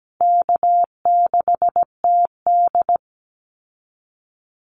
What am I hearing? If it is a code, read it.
Morse code, heard as a keyed tone, about 17 wpm, 700 Hz. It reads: K6TD
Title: Call Signs Repeated 3x - 17wpm